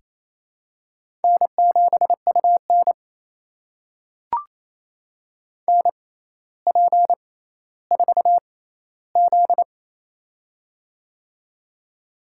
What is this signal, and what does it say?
Morse code, heard as a keyed tone, about 27 wpm, spaced as for 14 wpm, 700 Hz. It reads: N7UN NP4Z
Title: USA Call Signs - 4 Characters - 6x Character Spacing - 28wpm